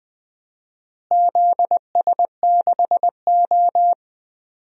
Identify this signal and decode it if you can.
Morse code, heard as a keyed tone, about 20 wpm, 700 Hz. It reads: ZS6O
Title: DXCC-100 Call Signs - 4 Characters - Repeated 3x - 20wpm